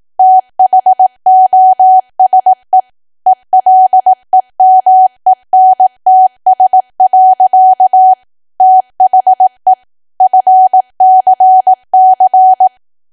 Morse code, heard as a keyed tone, about 18 wpm, 750 Hz. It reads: THOSE ELEMENTS. THE FCC
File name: cwa / W1AW-Level2 / 18WPM